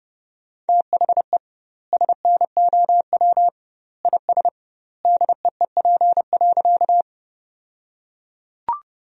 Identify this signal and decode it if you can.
Morse code, heard as a keyed tone, about 30 wpm, 700 Hz. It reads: THE SNOW IS DEEP.
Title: 2x Word-Spacing - Sentences from Top 500 Words - 30wpm